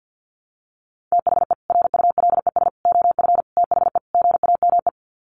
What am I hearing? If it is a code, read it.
Morse code, heard as a keyed tone, about 50 wpm, 700 Hz. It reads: THE RULES OF THE GAME
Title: Sets of 5 Words Encore2 50wpm